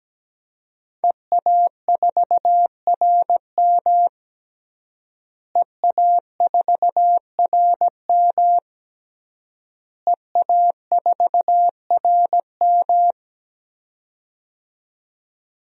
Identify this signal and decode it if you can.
Morse code, heard as a keyed tone, about 17 wpm, 700 Hz. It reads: EA4RM EA4RM EA4RM
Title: DXCC-100 Call Signs - 5 Characters - Repeated 3x - 17wpm